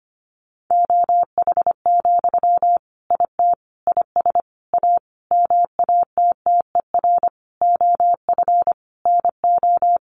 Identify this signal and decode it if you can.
Morse code, heard as a keyed tone, about 25 wpm, 700 Hz. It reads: OH, IT IS A MATTER OF NO
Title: Sentences from Top 1200 Words 25wpm